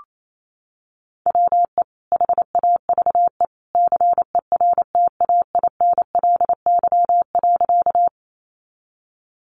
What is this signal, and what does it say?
Morse code, heard as a keyed tone, about 28 wpm, 690 Hz. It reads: WE HAVE CERTAINLY.
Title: Sentences from Top 900 Words 28wpm